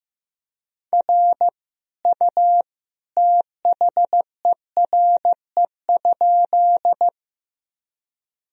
Morse code, heard as a keyed tone, about 16 wpm, 700 Hz. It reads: R U THERE?
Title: Sentences from Top 100 Words and Abbreviations 15wpm